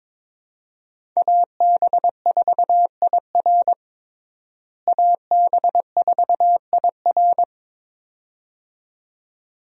Morse code, heard as a keyed tone, about 22 wpm, 700 Hz. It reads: AB4IR AB4IR
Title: USA Call Signs - 5 Characters - Repeated 2x - 22wpm